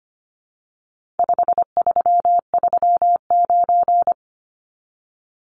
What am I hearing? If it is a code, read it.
Morse code, heard as a keyed tone, about 25 wpm, 700 Hz. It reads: 5339
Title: Numbers - 4-Digit 25wpm